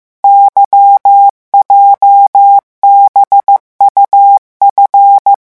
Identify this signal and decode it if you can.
Morse code, heard as a keyed tone, about 15 wpm, 800 Hz. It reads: YJBUF
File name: morse mp3 / revision8-vitesse 14 mots